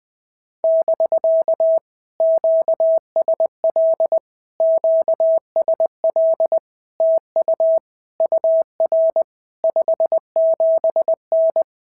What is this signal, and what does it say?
Morse code, heard as a keyed tone, about 20 wpm, 650 Hz. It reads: <BK> QSL QSL TU UR 57N
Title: POTA QSO Practice - 20wpm